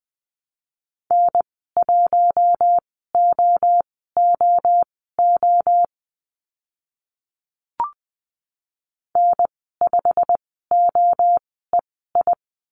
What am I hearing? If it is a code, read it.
Morse code, heard as a keyed tone, about 20 wpm, spaced as for 15 wpm, 700 Hz. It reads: N1OOO N5OEI
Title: 088 Lesson 15g - Number 5 - Call Signs 20wpm